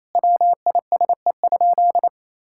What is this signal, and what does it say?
Morse code, heard as a keyed tone, about 28 wpm, 700 Hz. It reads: WISE?
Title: Sentences from Top 1600 Words 28wpm